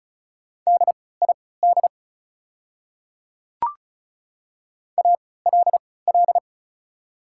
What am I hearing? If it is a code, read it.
Morse code, heard as a keyed tone, about 36 wpm, spaced as for 20 wpm, 690 Hz. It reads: DID ALL
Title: 3x Character Spacing - Top 100 Words - 35wpm